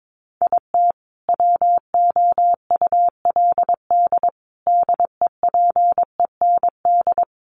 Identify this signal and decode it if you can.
Morse code, heard as a keyed tone, about 22 wpm, 700 Hz. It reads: IT WOULD DEPEND
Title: Sentences from Top 1300 Words 22wpm